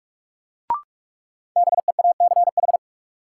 Morse code, heard as a keyed tone, about 42 wpm, 700 Hz. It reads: BEACH
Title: Sentences from Top 1500 Words - New Word Drill Alt - 45wpm